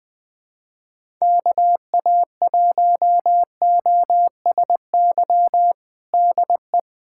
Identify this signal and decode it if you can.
Morse code, heard as a keyed tone, about 20 wpm, 700 Hz. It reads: KA1OSY DE
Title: USA Callsign DE Callsign - 6 Characters - 20wpm